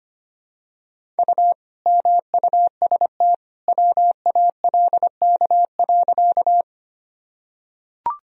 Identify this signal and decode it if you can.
Morse code, heard as a keyed tone, about 25 wpm, 700 Hz. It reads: U MUST WALK.
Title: Sentences from Top 300 Words and Abbreviations 25wpm